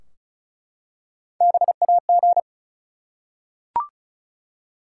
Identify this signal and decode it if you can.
Morse code, heard as a keyed tone, about 36 wpm, 700 Hz. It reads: BAG
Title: Three Letter Words - 35wpm